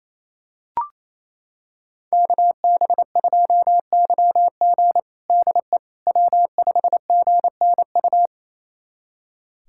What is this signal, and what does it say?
Morse code, heard as a keyed tone, about 27 wpm, 700 Hz. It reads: KB2YG DE W5GNU
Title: USA Callsign DE Callsign - 5 Characters - 28wpm